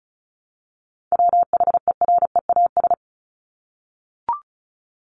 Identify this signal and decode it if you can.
Morse code, heard as a keyed tone, about 34 wpm, 700 Hz. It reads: WHEREAS
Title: Top 1501-1750 Words 35wpm